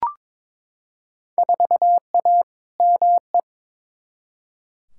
Morse code, heard as a keyed tone, about 21 wpm, 700 Hz. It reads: 4A ME
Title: Field Day Exchanges - 22wpm